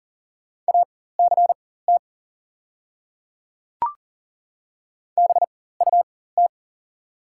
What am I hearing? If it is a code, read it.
Morse code, heard as a keyed tone, about 41 wpm, spaced as for 19 wpm, 700 Hz. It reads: ACT BUT
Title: 4x Character Spacing - Top 200 Words - 40wpm